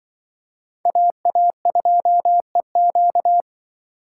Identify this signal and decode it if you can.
Morse code, heard as a keyed tone, about 23 wpm, 700 Hz. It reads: AA2EQ
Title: Speed Racing - USA 2x2 Call Signs 20wpm